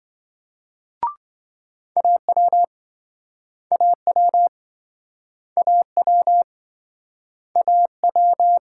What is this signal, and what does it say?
Morse code, heard as a keyed tone, about 27 wpm, 700 Hz. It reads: AW AW AW AW